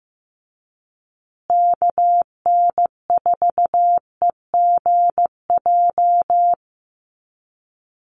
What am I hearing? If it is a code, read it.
Morse code, heard as a keyed tone, about 15 wpm, 700 Hz. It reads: KN4EGJ
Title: USA Call Signs Repeated 3x 15wpm